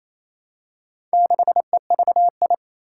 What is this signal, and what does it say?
Morse code, heard as a keyed tone, about 27 wpm, 700 Hz. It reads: 6EVI
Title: Four Challenging Letter-Number Combinations 28wpm